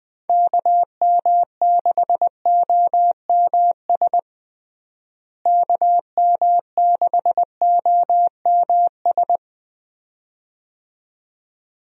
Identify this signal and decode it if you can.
Morse code, heard as a keyed tone, about 20 wpm, 700 Hz. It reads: KM6OMS KM6OMS
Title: USA Call Signs - 6 Characters - Repeated 2x - 20wpm